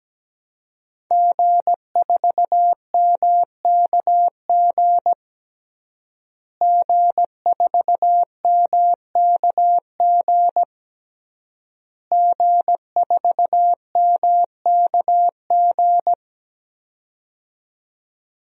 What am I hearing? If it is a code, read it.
Morse code, heard as a keyed tone, about 17 wpm, 700 Hz. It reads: G4MKG G4MKG G4MKG
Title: DXCC-100 Call Signs - 5 Characters - Repeated 3x - 17wpm